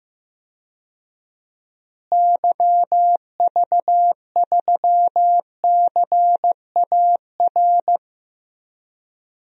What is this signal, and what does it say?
Morse code, heard as a keyed tone, about 15 wpm, 700 Hz. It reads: YV3CAR
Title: DXCC-100 Call Signs - 6 Characters - Repeated 2x - 15wpm